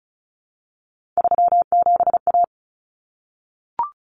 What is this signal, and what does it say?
Morse code, heard as a keyed tone, about 34 wpm, 700 Hz. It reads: 37A
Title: Common Cut Numbers - 3-Digit 35wpm